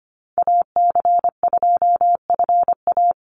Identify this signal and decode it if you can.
Morse code, heard as a keyed tone, about 25 wpm, 700 Hz. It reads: AC2FA
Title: Call Signs - 25wpm